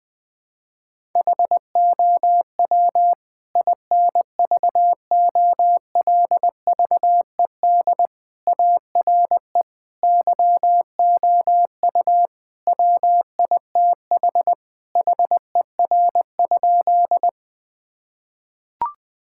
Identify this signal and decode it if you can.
Morse code, heard as a keyed tone, about 21 wpm, 700 Hz. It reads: HOW INVOLVED ARE YOU WITH HER?
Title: Sentences from Top 1000 Words - Review Part 5 - 20wpm